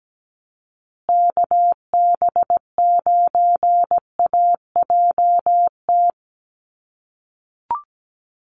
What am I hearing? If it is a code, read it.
Morse code, heard as a keyed tone, about 17 wpm, 700 Hz. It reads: KB9AJT
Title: USA Call Signs Repeated 3x 17wpm